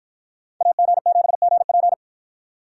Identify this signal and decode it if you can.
Morse code, heard as a keyed tone, about 51 wpm, 700 Hz. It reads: AG7GP